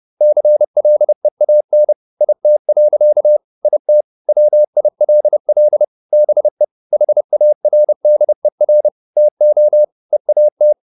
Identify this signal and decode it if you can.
Morse code, heard as a keyed tone, about 30 wpm, 600 Hz. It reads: CLEAN IT. IT WILL BE HARDER TO EAT